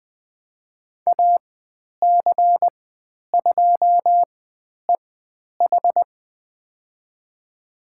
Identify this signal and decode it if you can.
Morse code, heard as a keyed tone, about 20 wpm, spaced as for 10 wpm, 700 Hz. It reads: AC2EH
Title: 095 Lesson 16g - Letter c - Call Signs 20wpm